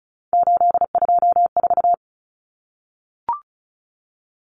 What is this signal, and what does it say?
Morse code, heard as a keyed tone, about 34 wpm, 700 Hz. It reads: 824
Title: Common Cut Numbers - 3-Digit 35wpm